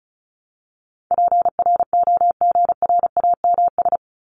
Morse code, heard as a keyed tone, about 34 wpm, 700 Hz. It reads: PROGRAMS
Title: ES:S Suffix - Top 100 Words with Base Words 35wpm